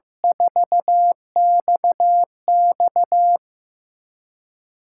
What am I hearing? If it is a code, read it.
Morse code, heard as a keyed tone, about 15 wpm, 700 Hz. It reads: 4XX
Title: USA Callsign DE Callsign - 5 Characters - 15wpm